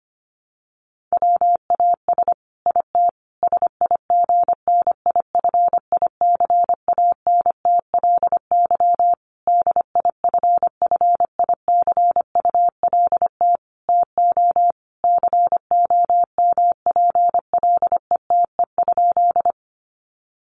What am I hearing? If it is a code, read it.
Morse code, heard as a keyed tone, about 25 wpm, 700 Hz. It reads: WAS IT SIGNIFICANTLY DIFFICULT TO COMPLETE?
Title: Sentences from Top 1500 Words 25wpm